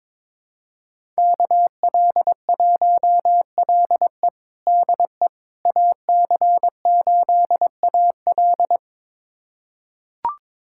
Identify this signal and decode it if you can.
Morse code, heard as a keyed tone, about 22 wpm, 700 Hz. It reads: KL1LE DE AC8AL
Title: USA Callsign DE Callsign - 5 Characters - 22wpm